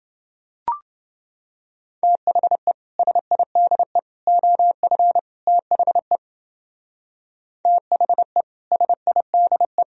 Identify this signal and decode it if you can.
Morse code, heard as a keyed tone, about 29 wpm, 700 Hz. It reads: THE SIDE OF THE THE SIDE